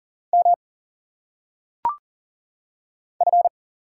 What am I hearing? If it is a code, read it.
Morse code, heard as a keyed tone, about 40 wpm, 700 Hz. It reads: M F
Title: Single Letters - Phonetic Pronunciation 40wpm